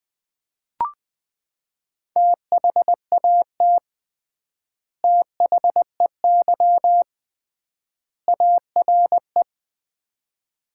Morse code, heard as a keyed tone, about 20 wpm, 700 Hz. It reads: THAT THEY ARE
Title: 3x Word-Spacing - Sets of 3 Words - 20wpm